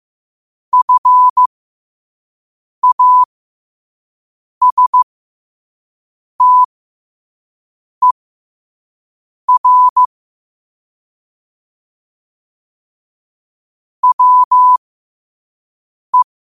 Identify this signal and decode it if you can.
Morse code, heard as a keyed tone, about 15 wpm, spaced as for 5 wpm, 1000 Hz. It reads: FASTER WE
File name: qotdfiles / 5wpm